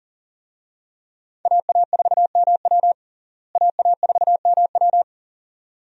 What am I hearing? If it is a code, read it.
Morse code, heard as a keyed tone, about 40 wpm, 700 Hz. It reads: AA4MW AA4MW
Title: USA Call Signs - 5 Characters - Repeated 2x - 40wpm